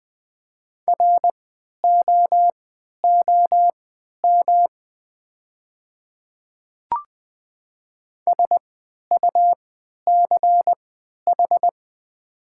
Morse code, heard as a keyed tone, about 20 wpm, spaced as for 11 wpm, 700 Hz. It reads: ROOM SUCH
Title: 3x Character Spacing - Top 300 Words - 20wpm